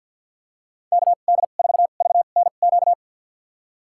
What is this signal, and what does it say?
Morse code, heard as a keyed tone, about 52 wpm, spaced as for 36 wpm, 700 Hz. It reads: KD4VNQ
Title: USA Call Signs - 2x Character Spacing - 50wpm